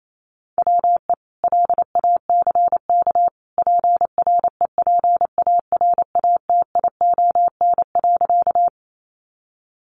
Morse code, heard as a keyed tone, about 28 wpm, 700 Hz. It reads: WE LACK PREPARATION.
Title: Sentences from Top 1400 Words 28wpm